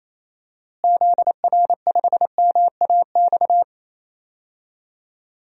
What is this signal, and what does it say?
Morse code, heard as a keyed tone, about 28 wpm, 700 Hz. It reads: ZR5MAX
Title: DXCC-100 Call Signs - 6 Characters - Repeated 3x - 28wpm